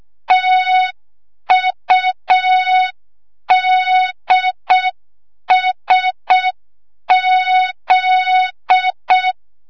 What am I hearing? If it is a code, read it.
Morse code, heard as a keyed tone, about 6 wpm, 3700 Hz. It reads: TUDSZ
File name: Morze-HA2MN / hang / 25 GYAKORLAT